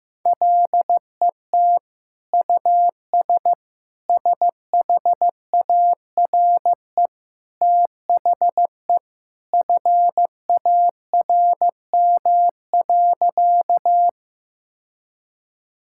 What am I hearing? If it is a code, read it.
Morse code, heard as a keyed tone, about 15 wpm, 700 Hz. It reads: LET US SHARE THE FARM.
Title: Sentences from Top 1000 Words - Review Part 6 - 15wpm